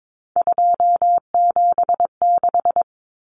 Morse code, heard as a keyed tone, about 22 wpm, 700 Hz. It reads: 276
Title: Numbers - 3-Digit 22wpm